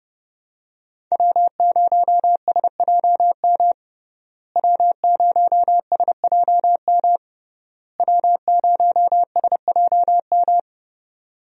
Morse code, heard as a keyed tone, about 30 wpm, 700 Hz. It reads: W0SJM W0SJM W0SJM
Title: USA Call Signs - 5 Characters - Repeated 3x - 30wpm